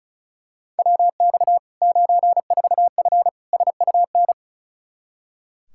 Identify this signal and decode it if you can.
Morse code, heard as a keyed tone, about 34 wpm, 700 Hz. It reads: WX 94F SUN
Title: Practice QSO - Exchange 2 - Info - 35wpm